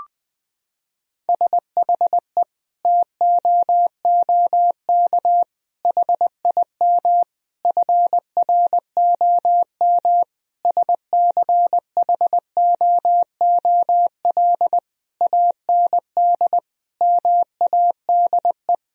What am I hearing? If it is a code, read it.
Morse code, heard as a keyed tone, about 21 wpm, 700 Hz. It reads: SHE TOOK HIM FROM SCHOOL AND MADE
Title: Sentences from Top 1000 Words - Review Part 4 - 20wpm